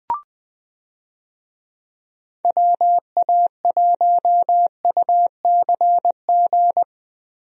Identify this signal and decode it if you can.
Morse code, heard as a keyed tone, about 20 wpm, 700 Hz. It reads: WA1UCG
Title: USA Call Signs - 6 Characters - Repeated 2x - 20wpm